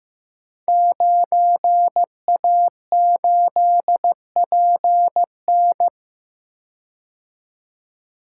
Morse code, heard as a keyed tone, about 15 wpm, 700 Hz. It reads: 9A8PN
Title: Code Groups - 5 Letter:Numbers - 15wpm